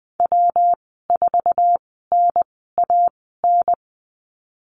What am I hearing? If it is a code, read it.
Morse code, heard as a keyed tone, about 20 wpm, spaced as for 15 wpm, 700 Hz. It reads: W4NAN
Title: 193 Lesson 31g - Letter v - Call Signs 20wpm